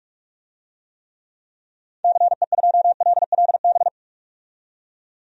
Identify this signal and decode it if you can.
Morse code, heard as a keyed tone, about 42 wpm, 700 Hz. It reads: CE2LLB
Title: DXCC-100 Call Signs - 6 Characters - 45wpm